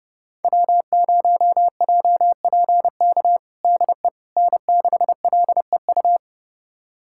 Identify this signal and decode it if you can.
Morse code, heard as a keyed tone, about 30 wpm, 700 Hz. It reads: W0JPK DE N6LEU
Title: USA Callsign DE Callsign - 5 Characters - 30wpm